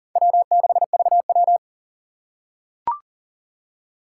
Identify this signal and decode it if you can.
Morse code, heard as a keyed tone, about 40 wpm, 700 Hz. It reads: W6VW
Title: USA Call Signs - 4 Characters - 4x Character Spacing - 40wpm